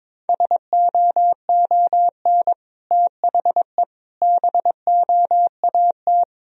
Morse code, heard as a keyed tone, about 21 wpm, 700 Hz. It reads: SOON THE BOAT
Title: Sentences from Top 1000 Words - Review Part 3 - 22wpm